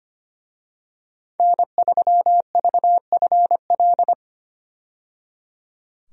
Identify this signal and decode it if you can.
Morse code, heard as a keyed tone, about 25 wpm, 700 Hz. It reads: N3VFL
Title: USA Call Signs - 5 Characters - 25wpm